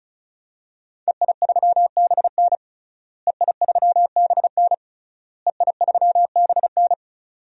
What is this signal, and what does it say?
Morse code, heard as a keyed tone, about 34 wpm, 700 Hz. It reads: EI3BN EI3BN EI3BN
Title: DXCC-100 Call Signs - 5 Characters - Repeated 3x - 35wpm